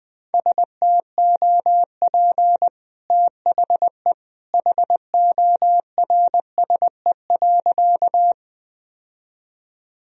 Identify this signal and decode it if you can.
Morse code, heard as a keyed tone, about 20 wpm, 700 Hz. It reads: STOP THE HORSE.
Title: Sentences from Top 1000 Words - Review Part 1 - 20wpm